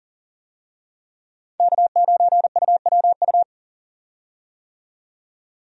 Morse code, heard as a keyed tone, about 40 wpm, 700 Hz. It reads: K9UWU
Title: USA Call Signs - 5 Characters - 40wpm